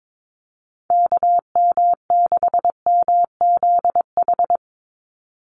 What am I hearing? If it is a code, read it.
Morse code, heard as a keyed tone, about 22 wpm, 700 Hz. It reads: KM6MZH
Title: USA Call Signs 22wpm